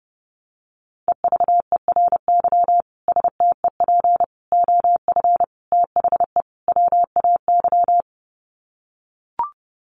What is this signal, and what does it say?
Morse code, heard as a keyed tone, about 30 wpm, 700 Hz. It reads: EVERY STEP OF THE WAY
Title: Sets of 5 Words Encore2 30wpm